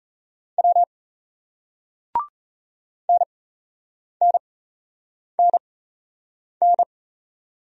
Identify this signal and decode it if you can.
Morse code, heard as a keyed tone, about 38 wpm, spaced as for 8 wpm, 700 Hz. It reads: W NNNN